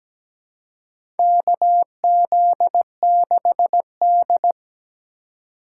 Morse code, heard as a keyed tone, about 17 wpm, 700 Hz. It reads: KZ6D
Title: USA Call Signs - 4 Characters - 4x Character Spacing - 17wpm